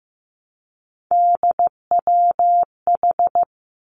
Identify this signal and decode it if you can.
Morse code, heard as a keyed tone, about 15 wpm, 700 Hz. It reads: DWH
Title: Three Challenging Letter-Number Combinations 15wpm